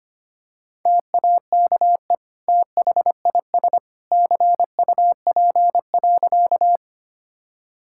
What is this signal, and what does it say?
Morse code, heard as a keyed tone, about 25 wpm, 700 Hz. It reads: TAKE THIS CUP.
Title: Sentences from Top 1600 Words 25wpm